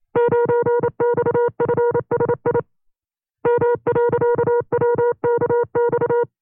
Morse code, heard as a keyed tone, about 28 wpm, 480 Hz. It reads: 9XFSI M.WKX
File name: FreeFiles / CW-practice / 25WPM-side2